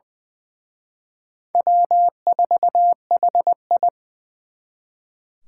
Morse code, heard as a keyed tone, about 23 wpm, 700 Hz. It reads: W4HI
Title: Speed Racing - USA 1x2 Call Signs 20wpm